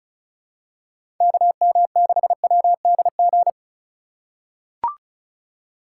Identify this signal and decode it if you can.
Morse code, heard as a keyed tone, about 36 wpm, 700 Hz. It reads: KM6WDG
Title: USA Call Signs - 6 Characters - 3x Character Spacing - 35wpm